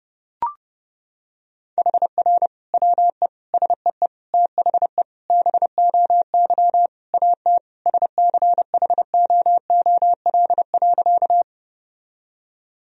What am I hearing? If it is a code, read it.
Morse code, heard as a keyed tone, about 30 wpm, 700 Hz. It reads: HR WE SEE THE BOY AT SCHOOL.
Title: Sentences from Top 300 Words and Abbreviations 30wpm